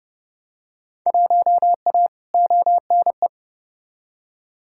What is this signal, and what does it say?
Morse code, heard as a keyed tone, about 31 wpm, 700 Hz. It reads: 1A ONE
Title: Field Day Exchanges - 30wpm